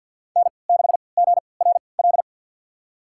Morse code, heard as a keyed tone, about 52 wpm, spaced as for 28 wpm, 700 Hz. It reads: N6GRL
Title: USA Call Signs - 5 Characters - 3x Character Spacing - 50wpm